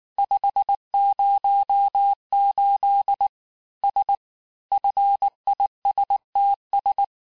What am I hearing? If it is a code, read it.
Morse code, heard as a keyed tone, about 19 wpm, 800 Hz. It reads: 508 S FISTS
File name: cours-du-lundi files / cours-1589-texte-FIN-de-QSO-vitesse-18mpm